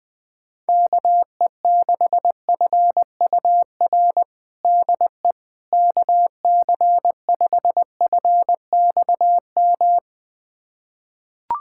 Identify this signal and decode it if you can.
Morse code, heard as a keyed tone, about 20 wpm, 700 Hz. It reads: KE6FUR DE KC5FXM
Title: USA Callsign DE Callsign 20wpm